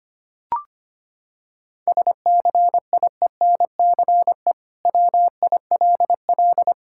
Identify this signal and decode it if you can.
Morse code, heard as a keyed tone, about 25 wpm, 700 Hz. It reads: SCIENCE WILL
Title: Sentences from Top 1000 Words - Review Part 6 - 25wpm